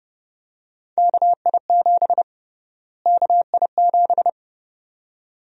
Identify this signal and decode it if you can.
Morse code, heard as a keyed tone, about 32 wpm, 700 Hz. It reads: KI7 KI7
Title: USA Call Sign Prefixes - Repeated 2x - 30wpm